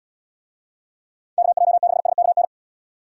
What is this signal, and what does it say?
Morse code, heard as a keyed tone, about 69 wpm, 700 Hz. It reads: KJ6SKN